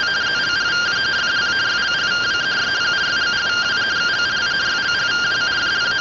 Two-tone teleprinter signal, about 170 Hz shift.